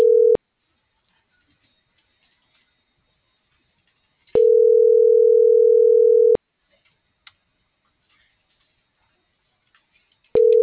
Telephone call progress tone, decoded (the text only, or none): ringback tone